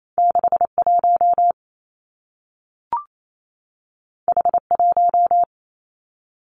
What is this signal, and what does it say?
Morse code, heard as a keyed tone, about 28 wpm, 700 Hz. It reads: 6 1 H 1
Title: Two Challenging Letter-Number Combinations 28wpm